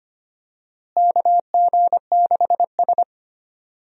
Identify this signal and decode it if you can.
Morse code, heard as a keyed tone, about 25 wpm, 700 Hz. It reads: KG6S
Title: USA Call Signs - 4 Characters - Repeated 3x - 25wpm